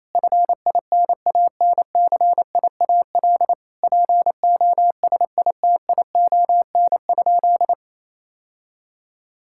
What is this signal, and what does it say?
Morse code, heard as a keyed tone, about 28 wpm, 700 Hz. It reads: FINANCIAL POSITION?
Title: Sentences from Top 1000 Words - Review Part 5 - 28wpm